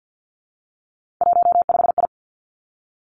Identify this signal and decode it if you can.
Morse code, heard as a keyed tone, about 50 wpm, 700 Hz. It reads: 15I
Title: Three Challenging Letter-Number Combinations 50wpm